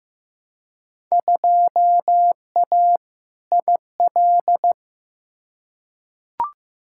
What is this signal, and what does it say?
Morse code, heard as a keyed tone, about 16 wpm, 700 Hz. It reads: 2A IL
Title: Field Day Exchanges - 15wpm